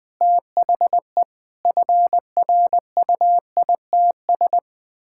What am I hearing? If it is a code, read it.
Morse code, heard as a keyed tone, about 20 wpm, 700 Hz. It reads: THE FRUITS
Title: Sentences with 6-Word Phrases - 20wpm